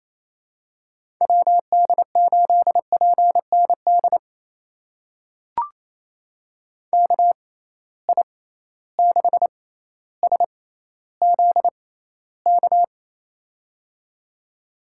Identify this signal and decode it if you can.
Morse code, heard as a keyed tone, about 28 wpm, spaced as for 14 wpm, 700 Hz. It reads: WD8PND KI6SZK
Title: USA Call Signs - 6 Characters - 6x Character Spacing - 28wpm